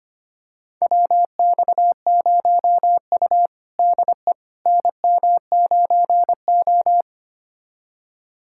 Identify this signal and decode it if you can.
Morse code, heard as a keyed tone, about 25 wpm, 700 Hz. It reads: WX0U DE NM9O
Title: USA Callsign DE Callsign - 4 Characters - 25wpm